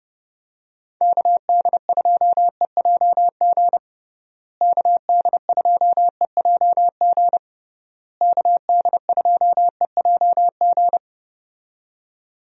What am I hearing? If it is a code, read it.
Morse code, heard as a keyed tone, about 32 wpm, 700 Hz. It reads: KD2EJG KD2EJG KD2EJG
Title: USA Call Signs - 6 Characters - Repeated 3x - 30wpm